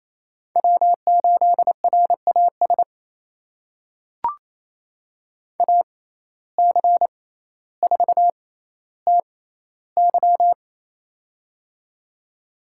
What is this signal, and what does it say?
Morse code, heard as a keyed tone, about 27 wpm, spaced as for 14 wpm, 700 Hz. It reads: W8RAS AC4TY
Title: USA Call Signs - 5 Characters - 6x Character Spacing - 28wpm